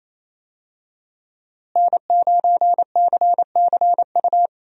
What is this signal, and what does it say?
Morse code, heard as a keyed tone, about 28 wpm, 700 Hz. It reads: N9CCU
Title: USA Call Signs - 5 Characters - Repeated 3x - 28wpm